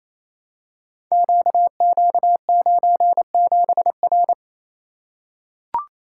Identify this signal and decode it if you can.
Morse code, heard as a keyed tone, about 28 wpm, 700 Hz. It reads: QQ97R
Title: Code Groups - 5 Letter:Numbers - 28wpm